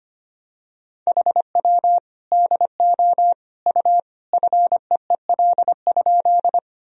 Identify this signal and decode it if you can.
Morse code, heard as a keyed tone, about 25 wpm, 700 Hz. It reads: HW DO U FEEL?
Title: Sentences from Top 400 Words and Abbreviations 25wpm